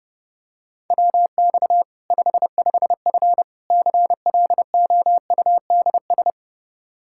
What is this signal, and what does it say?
Morse code, heard as a keyed tone, about 32 wpm, 700 Hz. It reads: WX 55F CLOUDS
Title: Practice QSO - Exchange 2 - Info - 30wpm